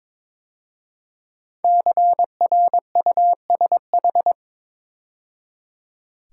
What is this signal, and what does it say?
Morse code, heard as a keyed tone, about 22 wpm, 700 Hz. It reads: CRUSH
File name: 5 Letter Words - First 3 Letters Given - 22wpm